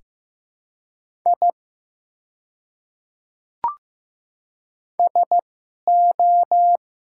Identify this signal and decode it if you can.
Morse code, heard as a keyed tone, about 15 wpm, spaced as for 11 wpm, 700 Hz. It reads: I SO
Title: 2x Character Spacing - Top 300 Words - 15wpm